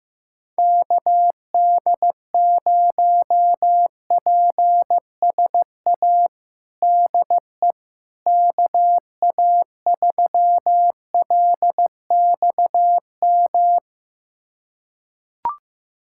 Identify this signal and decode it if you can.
Morse code, heard as a keyed tone, about 15 wpm, 700 Hz. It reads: KD0PSA DE KA3LXM
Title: USA Callsign DE Callsign - 6 Characters - 15wpm